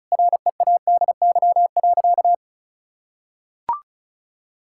Morse code, heard as a keyed tone, about 36 wpm, 700 Hz. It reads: READY.
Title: Sentences from Top 1600 Words 35wpm